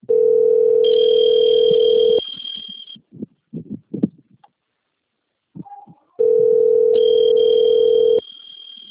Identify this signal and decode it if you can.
Ringback tone (North America), 440 + 480 Hz.